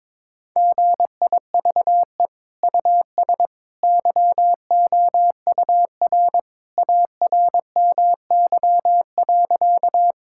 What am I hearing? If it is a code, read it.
Morse code, heard as a keyed tone, about 22 wpm, 700 Hz. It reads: GIVE US YOUR ARMY.
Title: Sentences from Top 1000 Words - Review Part 5 - 22wpm